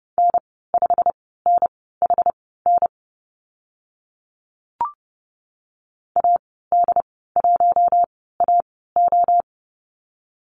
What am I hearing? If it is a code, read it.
Morse code, heard as a keyed tone, about 30 wpm, spaced as for 17 wpm, 700 Hz. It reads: N5NHN AD1AO
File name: Lesson 15g - Number 5 - Call Signs 30wpm